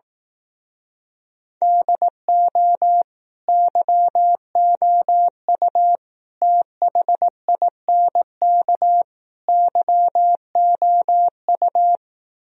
Morse code, heard as a keyed tone, about 18 wpm, 700 Hz. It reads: DO YOU THINK YOU
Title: Speed-Racing - Sets of 4 Words 15wpm